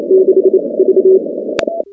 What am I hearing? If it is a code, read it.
Morse code, heard as a keyed tone, about 28 wpm, 380 Hz. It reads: 6 V
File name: contests / 2019-cw-wpx-kt5j-k5pi- -2330 0-16